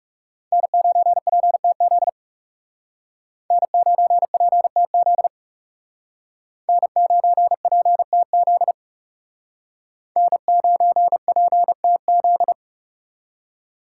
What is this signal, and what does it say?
Morse code, heard as a keyed tone, about 36 wpm, 700 Hz. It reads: N9PTZ N9PTZ N9PTZ N9PTZ
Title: Speed Racing - USA 1x3 Call Signs 30wpm